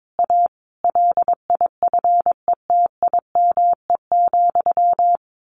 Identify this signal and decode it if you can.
Morse code, heard as a keyed tone, about 22 wpm, 700 Hz. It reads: A LIFETIME,
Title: A Sound Of Thunder 22wpm